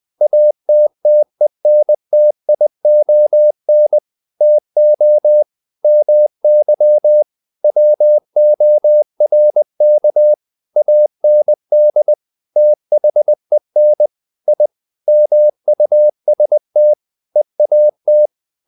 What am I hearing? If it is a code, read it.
Morse code, heard as a keyed tone, about 20 wpm, 600 Hz. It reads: ATTENTION TO MY WORK AND THEN I MUST EAT
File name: the old man and the sea / the old man and the sea part 1 20wpm